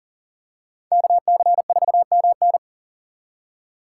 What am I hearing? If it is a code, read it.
Morse code, heard as a keyed tone, about 38 wpm, 700 Hz. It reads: KC4MN
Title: USA Call Signs - 5 Characters - 2x Character Spacing - 40wpm